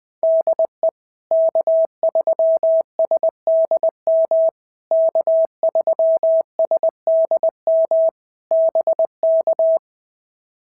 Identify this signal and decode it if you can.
Morse code, heard as a keyed tone, about 20 wpm, 650 Hz. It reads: DE K3SDM K3SDM BK
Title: POTA QSO Practice - 20wpm